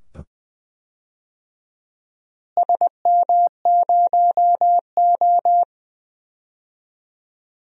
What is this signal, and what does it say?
Morse code, heard as a keyed tone, about 21 wpm, 700 Hz. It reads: SM0O
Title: DXCC-100 Call Signs - 4 Characters - Repeated 2x - 20wpm